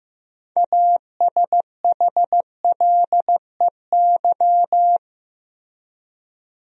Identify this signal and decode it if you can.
Morse code, heard as a keyed tone, about 15 wpm, 700 Hz. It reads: ASHLEY
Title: USA Common Names - 15wpm